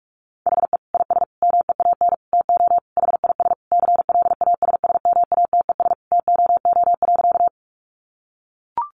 Morse code, heard as a keyed tone, about 44 wpm, 700 Hz. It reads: HE IS MEAN TO HIS CLASSMATES TOO.
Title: Additional Easy Phrases 45wpm